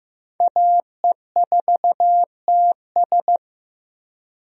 Morse code, heard as a keyed tone, about 15 wpm, 700 Hz. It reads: AE4TS
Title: USA Call Signs - 5 Characters - 15wpm